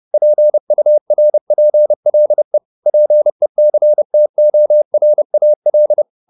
Morse code, heard as a keyed tone, about 30 wpm, 600 Hz. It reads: PURPLE PECTORAL
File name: the old man and the sea / the old man and the sea 1 30wpm